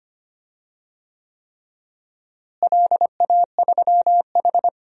Morse code, heard as a keyed tone, about 25 wpm, 700 Hz. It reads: LA3H
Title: DXCC-100 Call Signs - 4 Characters - Repeated 3x - 25wpm